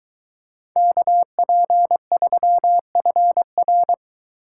Morse code, heard as a keyed tone, about 23 wpm, 700 Hz. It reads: KP3FR